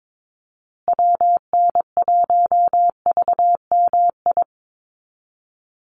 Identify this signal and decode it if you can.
Morse code, heard as a keyed tone, about 22 wpm, 700 Hz. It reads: WN1VMI
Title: USA Call Signs Repeated 3x 22wpm